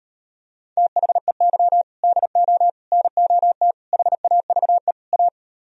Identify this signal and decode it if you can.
Morse code, heard as a keyed tone, about 38 wpm, 700 Hz. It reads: THEY DO NOT HAVE A